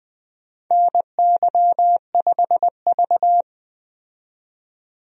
Morse code, heard as a keyed tone, about 20 wpm, 700 Hz. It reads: NY5V
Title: USA Call Signs - 4 Characters - 20wpm